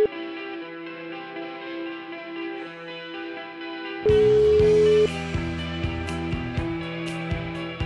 Ringback tone (Europe, 425 Hz).